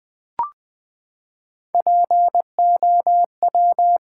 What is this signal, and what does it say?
Morse code, heard as a keyed tone, about 20 wpm, 700 Hz. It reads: POW
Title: Build A Word - 3 Letters At A Time - 6-Letter Words - 20wpm